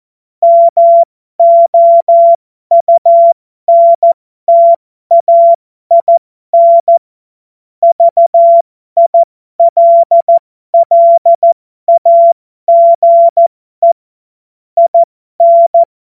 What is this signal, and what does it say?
Morse code, heard as a keyed tone, about 14 wpm, spaced as for 12 wpm, 680 Hz. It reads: MOUNTAIN VILLAGE IN
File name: Morsekurs / Witze